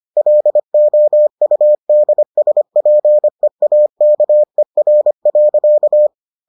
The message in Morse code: LOUDSPEAKER.

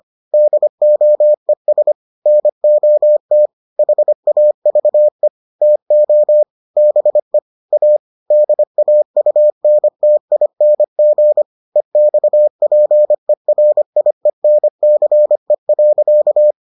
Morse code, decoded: DOES NOT HAVE TO BE A DAUNTING EXPERIENCE.

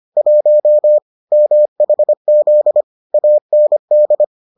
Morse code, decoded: 1 MHZ AND

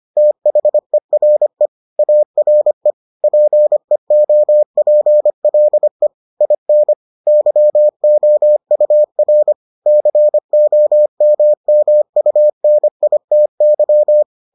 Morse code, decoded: THERE ARE PEOPLE IN YOUR COMMUNITY

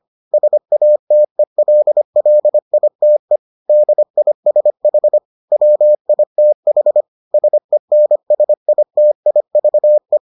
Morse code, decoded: SATELLITE DISH WITH SENSITIVE

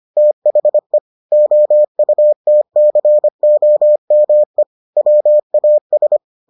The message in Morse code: THE OUTCOME WAS

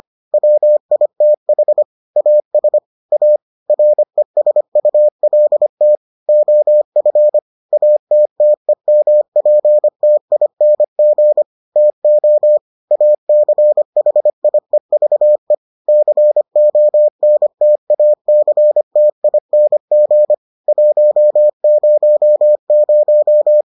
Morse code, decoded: WITH AS A RESULT OF ATTEMPTING TO ACHIEVE CONTACTING 100

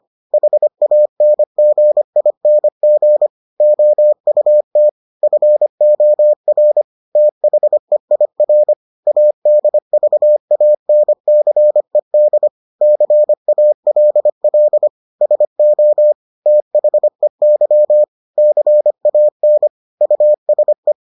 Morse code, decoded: HANGING OUT FOR THEIR ADVANCED CALL SO THEY CAN USE

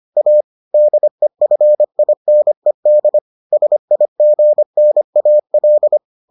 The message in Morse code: A DEFINED SIGNAL

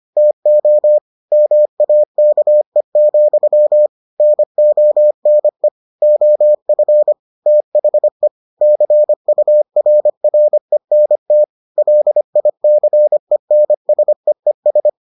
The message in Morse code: TO MAKE, NONE OF THE CURRENT LICENSEES

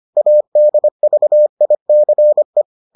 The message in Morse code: ADVICE

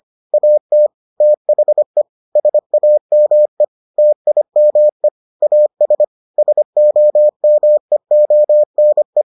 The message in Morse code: AT THE SAME TIME AS SOMEONE

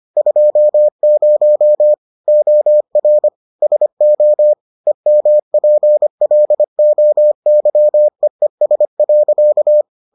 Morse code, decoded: 20 OR SO EMPLOYEES.

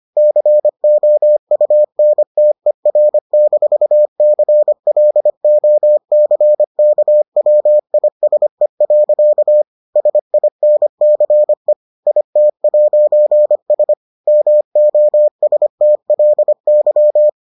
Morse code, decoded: COUNTER-CLOCKWISE. SINCE IT'S MOSTLY

